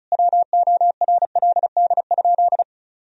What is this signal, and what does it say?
Morse code, heard as a keyed tone, about 36 wpm, 700 Hz. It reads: WORLD?
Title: Sentences from Top 200 Words and Abbreviations 35wpm